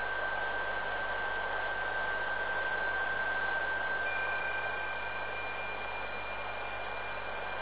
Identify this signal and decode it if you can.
PSK31, about 31.25 baud.